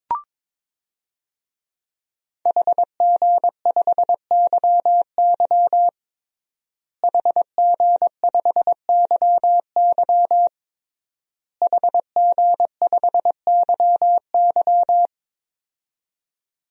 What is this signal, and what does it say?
Morse code, heard as a keyed tone, about 22 wpm, 700 Hz. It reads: HG5YY HG5YY HG5YY
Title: DXCC-100 Call Signs - 5 Characters - Repeated 3x - 22wpm